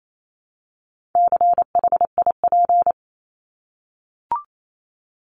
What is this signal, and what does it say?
Morse code, heard as a keyed tone, about 28 wpm, 700 Hz. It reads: CHIP
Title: Top 1751-2000 Words 28wpm